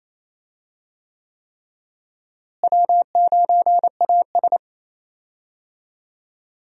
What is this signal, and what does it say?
Morse code, heard as a keyed tone, about 28 wpm, 700 Hz. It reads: W9AS
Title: USA Call Signs - 4 Characters - Repeated 3x - 28wpm